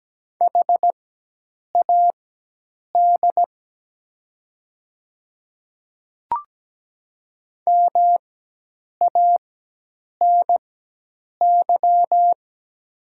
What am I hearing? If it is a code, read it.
Morse code, heard as a keyed tone, about 17 wpm, spaced as for 8 wpm, 700 Hz. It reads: HAD MANY
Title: 4x Character Spacing - Top 100 Words - 17wpm